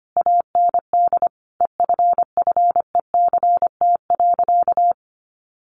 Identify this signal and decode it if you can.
Morse code, heard as a keyed tone, about 25 wpm, 700 Hz. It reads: AND EFFECT.
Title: Sentences from Top 1000 Words 25wpm